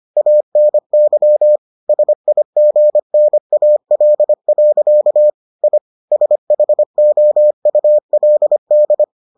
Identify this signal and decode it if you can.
Morse code, heard as a keyed tone, about 25 wpm, 600 Hz. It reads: ANY SIGNAL. I SHOULD